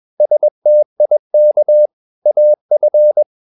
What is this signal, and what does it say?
Morse code, heard as a keyed tone, about 21 wpm, 600 Hz. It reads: STIK AF
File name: Ferie QRS 21wpm